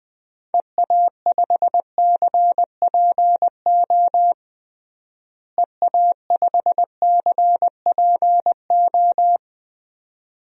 Morse code, heard as a keyed tone, about 20 wpm, 700 Hz. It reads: EA5CPO EA5CPO
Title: DXCC-100 Call Signs - 6 Characters - Repeated 2x - 20wpm